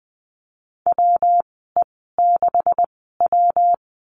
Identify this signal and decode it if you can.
Morse code, heard as a keyed tone, about 20 wpm, spaced as for 14 wpm, 700 Hz. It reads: WE6W
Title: 161 Lesson 26g - Letter g - Call Signs 20wpm